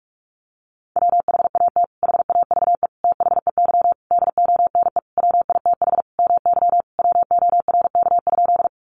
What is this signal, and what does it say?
Morse code, heard as a keyed tone, about 44 wpm, 700 Hz. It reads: WHAT HAVE THEY DONE WITH MY WORK?
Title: Sentences from Top 500 Words 45wpm